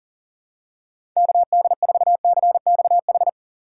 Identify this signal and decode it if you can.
Morse code, heard as a keyed tone, about 40 wpm, 700 Hz. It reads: KD4CXH
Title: USA Call Signs - 6 Characters - 4x Character Spacing - 40wpm